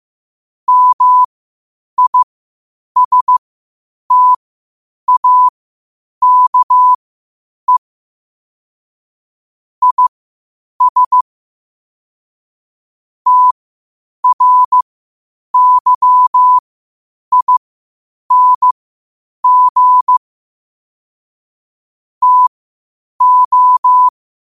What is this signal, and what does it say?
Morse code, heard as a keyed tone, about 15 wpm, spaced as for 8 wpm, 1000 Hz. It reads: MISTAKE IS TRYING TO
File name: qotdfiles / 8wpm